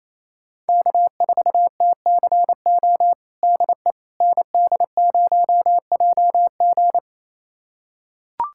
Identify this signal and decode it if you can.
Morse code, heard as a keyed tone, about 28 wpm, 700 Hz. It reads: K4TCO DE ND0JG
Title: USA Callsign DE Callsign - 5 Characters - 28wpm